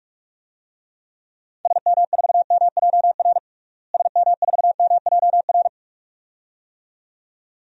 Morse code, heard as a keyed tone, about 46 wpm, 690 Hz. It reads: SM4MJR SM4MJR
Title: DXCC-100 Call Signs - 6 Characters - Repeated 2x - 45wpm